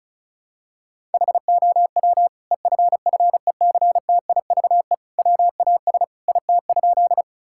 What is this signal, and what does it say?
Morse code, heard as a keyed tone, about 36 wpm, 700 Hz. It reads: HOW EFFECTIVE WAS IT?
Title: Sentences from Top 1000 Words - Review Part 6 - 35wpm